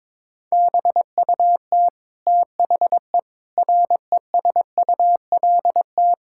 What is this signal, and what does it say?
Morse code, heard as a keyed tone, about 22 wpm, 700 Hz. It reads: BUT THE RESULT
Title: Sentences from Top 1000 Words - Review Part 5 - 22wpm